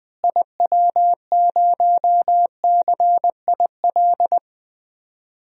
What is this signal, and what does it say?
Morse code, heard as a keyed tone, about 20 wpm, 700 Hz. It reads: IW0CIL
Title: DXCC-100 Call Signs - 6 Characters - Repeated 2x - 20wpm